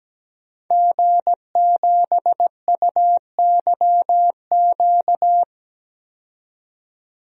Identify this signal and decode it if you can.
Morse code, heard as a keyed tone, about 17 wpm, 700 Hz. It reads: G7UYQ
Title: DXCC-100 Call Signs - 5 Characters - 17wpm